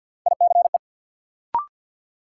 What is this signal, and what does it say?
Morse code, heard as a keyed tone, about 53 wpm, 700 Hz. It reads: ICE
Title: Sentences from Top 1000 Words - New Word Drill - 50wpm